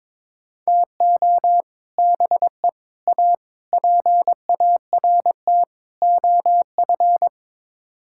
Morse code, heard as a keyed tone, about 22 wpm, 700 Hz. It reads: TO BE A PART OF
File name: Build A Phrase - 5-Word Phrases - 22wpm